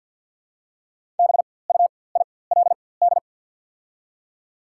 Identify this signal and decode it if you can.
Morse code, heard as a keyed tone, about 48 wpm, spaced as for 23 wpm, 700 Hz. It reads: BUILD
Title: 4x Character Spacing - Top 300 Words - 50wpm